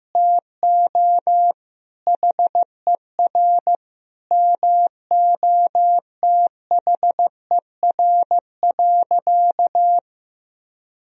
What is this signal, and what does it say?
Morse code, heard as a keyed tone, about 15 wpm, 700 Hz. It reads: TO HER MOTHER.
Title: Sentences from Top 1000 Words - Review Part 4 - 15wpm